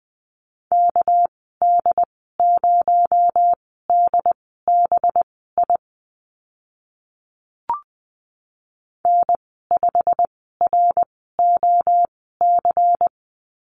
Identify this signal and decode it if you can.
Morse code, heard as a keyed tone, about 20 wpm, spaced as for 14 wpm, 700 Hz. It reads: KD0DBI N5ROC
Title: 245 Lesson 39g - Letter z - Call Signs 20wpm